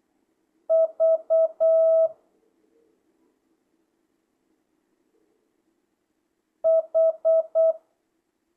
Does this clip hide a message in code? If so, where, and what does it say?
Yes: Morse code from 0.7 to 7.7 s, heard as a keyed tone, about 8 wpm, 650 Hz. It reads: V H